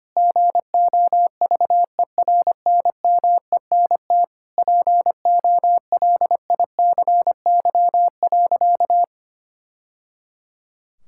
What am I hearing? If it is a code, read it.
Morse code, heard as a keyed tone, about 25 wpm, 700 Hz. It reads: GOVERNMENT POLICY.
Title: Sentences from Top 1000 Words - Review Part 5 - 25wpm